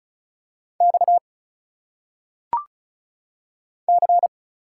Morse code, heard as a keyed tone, about 35 wpm, 700 Hz. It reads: X C
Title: Single Letters - Phonetic Pronunciation 35wpm